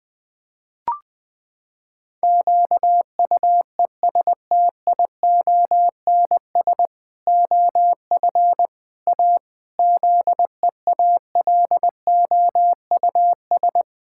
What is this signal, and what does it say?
Morse code, heard as a keyed tone, about 20 wpm, 700 Hz. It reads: QUESTIONS OF A ZEALOUS
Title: Pangram Challenge v2 - 20wpm